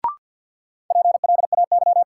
Morse code, heard as a keyed tone, about 53 wpm, 700 Hz. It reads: PLAY
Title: Sentences from Top 200 Words - New Word Drill - 50wpm